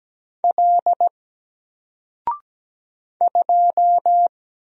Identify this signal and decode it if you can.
Morse code, heard as a keyed tone, about 17 wpm, 700 Hz. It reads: L 2
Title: Single Character (Letter:Number:Punctuation:Prosign) 17wpm